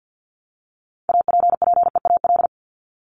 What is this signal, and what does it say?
Morse code, heard as a keyed tone, about 50 wpm, 700 Hz. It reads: APPEAL
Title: Top 1251-1500 Words 50wpm